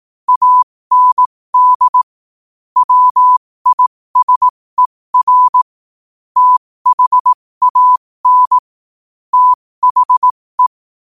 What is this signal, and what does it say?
Morse code, heard as a keyed tone, about 18 wpm, spaced as for 15 wpm, 1000 Hz. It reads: AND WISER THAN THE
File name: 15wpm